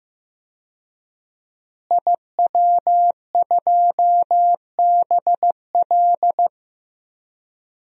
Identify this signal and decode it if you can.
Morse code, heard as a keyed tone, about 15 wpm, 700 Hz. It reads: IW2BL
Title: DXCC-100 Call Signs - 5 Characters - Repeated 3x - 15wpm